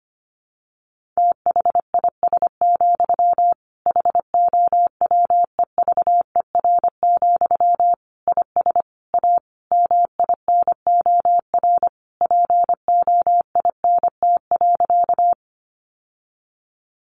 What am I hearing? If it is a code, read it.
Morse code, heard as a keyed tone, about 25 wpm, 700 Hz. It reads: THIS, HOWEVER, IS A MINOR POINT.
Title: Sentences from Top 1500 Words 25wpm